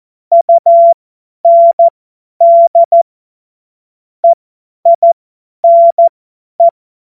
Morse code, heard as a keyed tone, about 14 wpm, spaced as for 10 wpm, 680 Hz. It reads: UND EINE
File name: Morsekurs / Witze / Witz 30 Januar 2024 50 F70